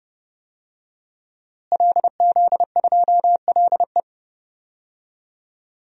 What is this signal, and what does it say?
Morse code, heard as a keyed tone, about 30 wpm, 700 Hz. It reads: LZ2LE
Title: DXCC-100 Call Signs - 5 Characters - Repeated 2x - 30wpm